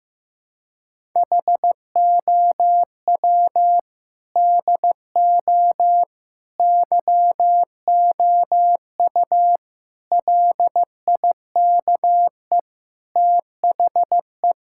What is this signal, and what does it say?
Morse code, heard as a keyed tone, about 15 wpm, 700 Hz. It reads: HOW DO YOU LIKE THE
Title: Sentences from Top 1000 Words - Review Part 6 - 15wpm